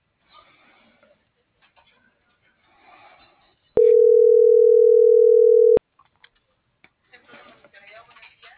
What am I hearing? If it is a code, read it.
Ringback tone (North America), 440 + 480 Hz.
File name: CENTRAL MAYAB / out lizbeth.mayab - .28307